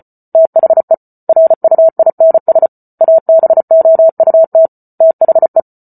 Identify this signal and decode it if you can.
Morse code, heard as a keyed tone, about 34 wpm, 650 Hz. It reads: THE RUINS ABOUT THE